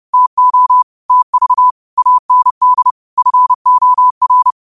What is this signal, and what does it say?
Morse code, heard as a keyed tone, about 30 wpm, 1000 Hz. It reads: TO TV AND FOR
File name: qotdfiles / 30wpm